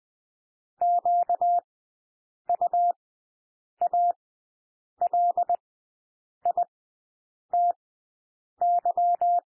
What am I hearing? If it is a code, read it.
Morse code, heard as a keyed tone, about 20 wpm, spaced as for 8 wpm, 700 Hz. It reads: QUALITY
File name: cwmp3 / W1 SOUND WORDS 20 wpm